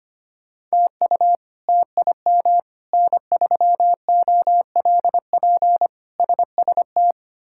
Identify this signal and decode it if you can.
Morse code, heard as a keyed tone, about 25 wpm, 700 Hz. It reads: TU TIM N3OLP SST
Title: K1USN SST QSO Practice - 25wpm